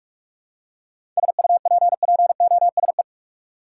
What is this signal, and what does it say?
Morse code, heard as a keyed tone, about 41 wpm, 700 Hz. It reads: SUPPOSE
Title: Seven Letter Words - 45wpm